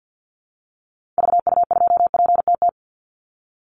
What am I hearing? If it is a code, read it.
Morse code, heard as a keyed tone, about 49 wpm, 700 Hz. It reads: VU2PTT
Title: Call Signs Repeated 3x - 50wpm